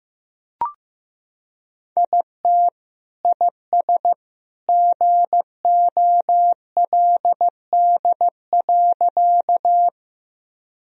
Morse code, heard as a keyed tone, about 15 wpm, 700 Hz. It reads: IT IS GOLD.
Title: Sentences from Top 1000 Words - Review Part 3 - 15wpm